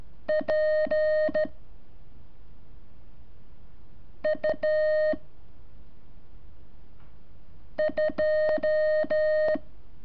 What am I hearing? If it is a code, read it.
Morse code, heard as a keyed tone, about 13 wpm, 620 Hz. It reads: P U 2
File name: curso / ejercicio 26